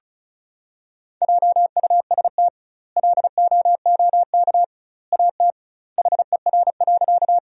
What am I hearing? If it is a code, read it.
Morse code, heard as a keyed tone, about 34 wpm, 700 Hz. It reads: JUST LOOK AT HER.
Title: 2x Word-Spacing - Sentences from Top 200 Words - 35wpm